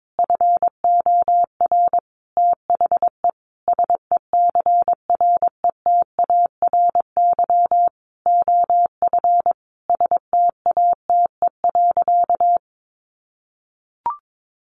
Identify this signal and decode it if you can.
Morse code, heard as a keyed tone, about 22 wpm, 700 Hz. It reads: FOR THE SECRETARY OF STATE.
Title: Sentences from Top 1400 Words 22wpm